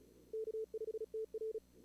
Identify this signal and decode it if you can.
Morse code, heard as a keyed tone, about 39 wpm, 440 Hz. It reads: K5TR
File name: CWSprint / Feb2022 / k5tr-feb-na-sprint-cw-k5ot -2300